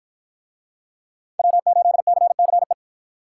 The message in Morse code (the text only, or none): W8CCE